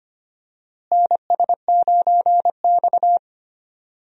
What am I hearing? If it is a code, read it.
Morse code, heard as a keyed tone, about 25 wpm, 700 Hz. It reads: NS9X
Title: USA Call Signs - 4 Characters - Repeated 2x - 25wpm